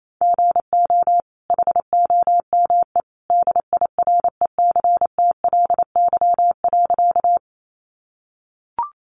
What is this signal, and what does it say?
Morse code, heard as a keyed tone, about 28 wpm, 700 Hz. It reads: GO HOME DIRECTLY.
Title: Sentences from Top 800 Words 28wpm